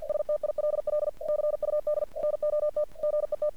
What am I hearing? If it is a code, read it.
Morse code, heard as a keyed tone, about 49 wpm, 600 Hz. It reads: STILL CUD NOT GET
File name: archived / CRSnet